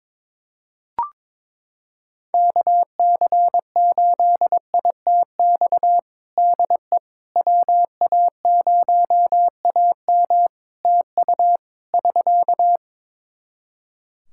Morse code, heard as a keyed tone, about 22 wpm, 700 Hz. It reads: KC8ITX DE WA0AM TU <SK>
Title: Practice QSO - Exchange 4 - Ending - 22wpm